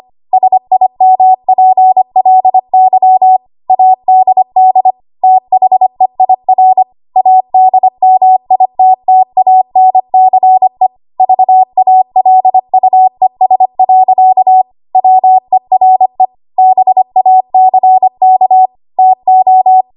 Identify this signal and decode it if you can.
Morse code, heard as a keyed tone, about 25 wpm, 750 Hz. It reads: SIMPLY ADD THEIR ADMITTANCE VALUES. WERE BACK TO
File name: Archive / 25 WPM / 25WPM